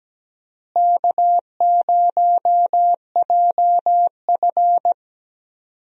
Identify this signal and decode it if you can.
Morse code, heard as a keyed tone, about 17 wpm, 700 Hz. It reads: K0JF
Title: USA Call Signs - 4 Characters - Repeated 2x - 17wpm